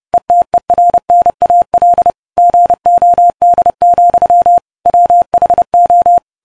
Morse code, heard as a keyed tone, about 30 wpm, 700 Hz. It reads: ETERNAL GOD, WHO